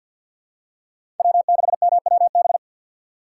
Morse code, heard as a keyed tone, about 53 wpm, 700 Hz. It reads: W6MWB
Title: USA Call Signs - 5 Characters - 3x Character Spacing - 50wpm